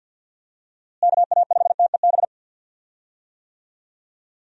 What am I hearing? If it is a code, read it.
Morse code, heard as a keyed tone, about 46 wpm, 700 Hz. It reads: KA5TEB
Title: USA Call Signs - 6 Characters - Repeated 2x - 50wpm